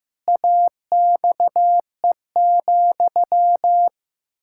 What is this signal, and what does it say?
Morse code, heard as a keyed tone, about 15 wpm, 700 Hz. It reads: AXE,
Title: Pangram Challenge v2 - 15wpm